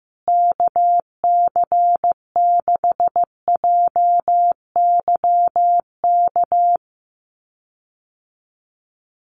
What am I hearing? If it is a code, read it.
Morse code, heard as a keyed tone, about 15 wpm, 700 Hz. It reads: KC6JYK
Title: USA Call Signs Repeated 3x 15wpm